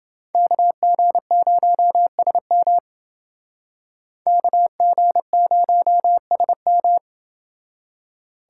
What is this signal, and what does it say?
Morse code, heard as a keyed tone, about 28 wpm, 700 Hz. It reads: KG0SM KG0SM